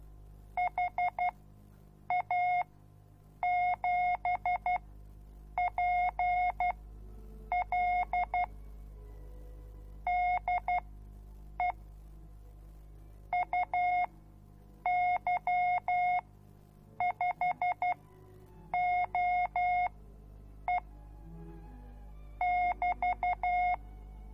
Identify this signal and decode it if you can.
Morse code, heard as a keyed tone, about 12 wpm, spaced as for 7 wpm, 720 Hz. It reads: HA7PL DE UY5OE <BT>